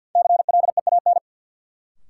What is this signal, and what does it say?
Morse code, heard as a keyed tone, about 53 wpm, 700 Hz. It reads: CLEAN
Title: Sentences from Top 1100 Words - New Word Drill - 50wpm